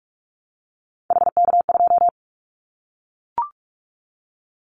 Morse code, heard as a keyed tone, about 44 wpm, 700 Hz. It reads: HK2
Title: Callsign Prefixes for Top 100 DXCC - 45wpm